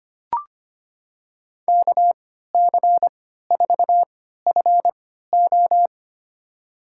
Morse code, heard as a keyed tone, about 25 wpm, spaced as for 14 wpm, 700 Hz. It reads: KC4FO
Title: USA Call Signs - 5 Characters - 3x Character Spacing - 25wpm